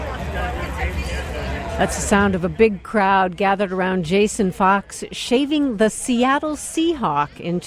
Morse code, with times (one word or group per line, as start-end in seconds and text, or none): none